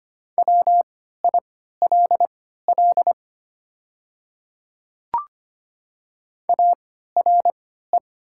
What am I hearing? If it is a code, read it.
Morse code, heard as a keyed tone, about 25 wpm, spaced as for 14 wpm, 700 Hz. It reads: WILL ARE
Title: 3x Character Spacing - Top 100 Words - 25wpm